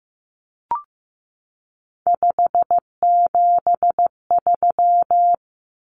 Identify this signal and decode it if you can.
Morse code, heard as a keyed tone, about 15 wpm, 700 Hz. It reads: 573
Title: Numbers - 3-Digit 15wpm